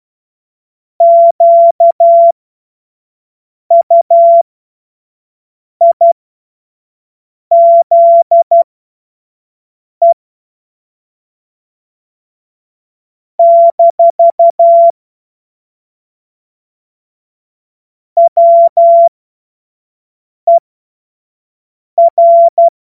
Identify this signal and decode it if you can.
Morse code, heard as a keyed tone, about 12 wpm, spaced as for 5 wpm, 680 Hz. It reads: QUIZE - WER
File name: Morsekurs / Witze / Witz 07. Juni 2022 25 F70